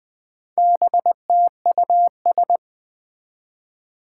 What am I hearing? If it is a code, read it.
Morse code, heard as a keyed tone, about 20 wpm, 700 Hz. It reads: BTUS
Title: Four Challenging Letter-Number Combinations 20wpm